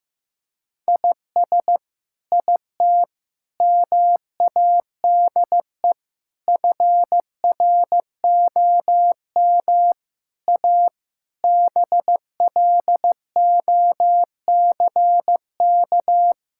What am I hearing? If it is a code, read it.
Morse code, heard as a keyed tone, about 16 wpm, 700 Hz. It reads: IS IT MADE FROM A BLOCK
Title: Sentences from Top 1600 Words 15wpm